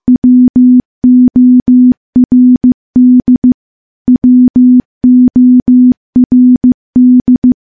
Morse code, heard as a keyed tone, about 15 wpm, 250 Hz. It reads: WORD WORD
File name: morse / morse-training-15WPM